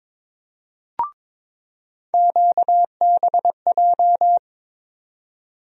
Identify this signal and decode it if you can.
Morse code, heard as a keyed tone, about 22 wpm, 700 Hz. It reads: QBJ
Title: USA Call Sign Postfixes - 3 Letters 22wpm